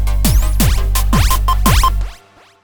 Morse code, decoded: none